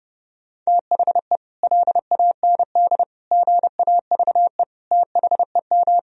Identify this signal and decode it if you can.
Morse code, heard as a keyed tone, about 30 wpm, 700 Hz. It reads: THE LAND GAVE THEM
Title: Sentences from Top 1000 Words - Review Part 1 - 30wpm